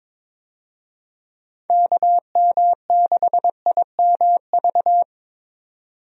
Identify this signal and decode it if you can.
Morse code, heard as a keyed tone, about 23 wpm, 700 Hz. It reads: KM6IMV
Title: USA Call Signs - 6 Characters - 22wpm